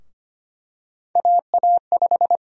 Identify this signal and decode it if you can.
Morse code, heard as a keyed tone, about 25 wpm, 700 Hz. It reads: AA5
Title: USA Call Sign Prefixes - 25wpm